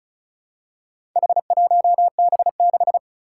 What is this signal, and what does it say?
Morse code, heard as a keyed tone, about 36 wpm, 690 Hz. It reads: H1B6
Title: Four Challenging Letter-Number Combinations 35wpm